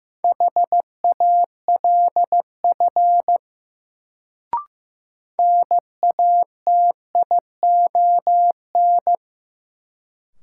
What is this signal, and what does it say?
Morse code, heard as a keyed tone, about 15 wpm, 700 Hz. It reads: HALF NATION
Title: Sets of 3 Words - New Word Drill - 15wpm